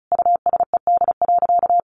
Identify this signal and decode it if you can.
Morse code, heard as a keyed tone, about 34 wpm, 700 Hz. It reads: USED.
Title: Sentences from Top 600 Words 35wpm